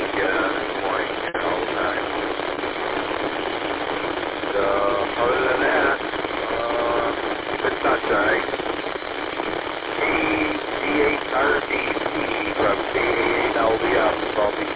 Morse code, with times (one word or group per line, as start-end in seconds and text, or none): none